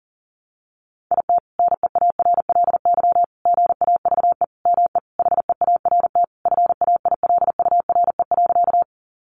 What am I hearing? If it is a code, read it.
Morse code, heard as a keyed tone, about 40 wpm, 700 Hz. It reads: IT NEARLY GAVE ME HEART FAILURE.
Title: Sentences from Top 1300 Words 40wpm